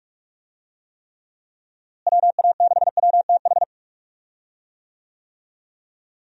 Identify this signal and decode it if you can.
Morse code, heard as a keyed tone, about 42 wpm, 700 Hz. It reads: WA6WTH
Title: USA Call Signs - 6 Characters - Repeated 3x - 45wpm